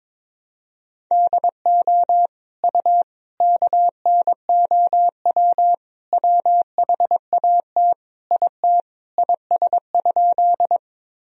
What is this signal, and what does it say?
Morse code, heard as a keyed tone, about 22 wpm, 700 Hz. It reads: DO U KNOW WHAT IT IS?
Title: Sentences from Top 100 Words and Abbreviations 22wpm